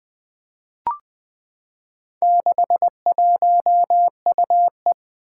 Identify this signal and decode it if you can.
Morse code, heard as a keyed tone, about 20 wpm, 700 Hz. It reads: 61UE
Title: Four Challenging Letter-Number Combinations 20wpm